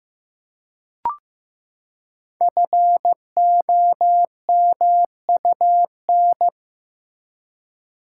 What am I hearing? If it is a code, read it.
Morse code, heard as a keyed tone, about 15 wpm, 700 Hz. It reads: FOMUN
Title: Code Groups - 5 Letters - 15wpm